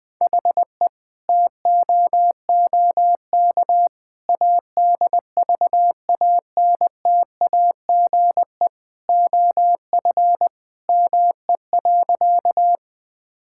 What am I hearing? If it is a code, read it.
Morse code, heard as a keyed tone, about 20 wpm, 700 Hz. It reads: HE TOOK ADVANTAGE OF ME.
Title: Sentences from Top 1000 Words - Review Part 6 - 20wpm